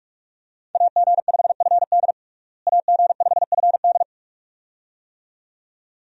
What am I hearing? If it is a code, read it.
Morse code, heard as a keyed tone, about 46 wpm, 700 Hz. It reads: AG5FD AG5FD
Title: USA Call Signs - 5 Characters - Repeated 2x - 45wpm